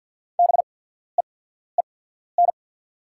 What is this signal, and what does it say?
Morse code, heard as a keyed tone, about 48 wpm, spaced as for 14 wpm, 700 Hz. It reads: BEEN
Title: 8x Character Spacing - Top 100 Words - 50wpm